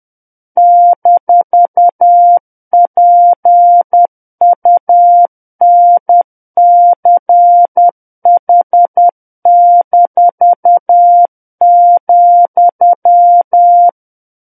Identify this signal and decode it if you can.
Morse code, heard as a keyed tone, about 10 wpm, 700 Hz. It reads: -PUNCH-,
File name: Book One / 01-Chapter 1 - 10 WPM